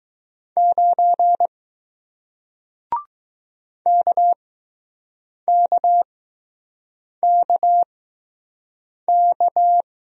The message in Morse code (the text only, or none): 9 KKKK